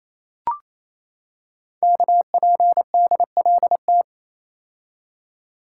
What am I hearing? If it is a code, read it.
Morse code, heard as a keyed tone, about 27 wpm, 700 Hz. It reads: KPDLT
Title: Code Groups - 5 Letters - 28wpm